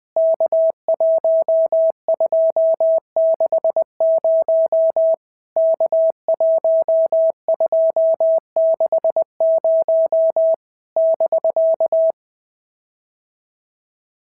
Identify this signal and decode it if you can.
Morse code, heard as a keyed tone, about 19 wpm, 650 Hz. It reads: K1260 K1260 <BK>
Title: POTA QSO Practice - 20wpm